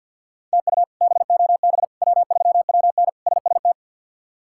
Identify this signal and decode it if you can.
Morse code, heard as a keyed tone, about 52 wpm, 700 Hz. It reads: TU BOB W3WN SST
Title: K1USN SST QSO Practice - 50wpm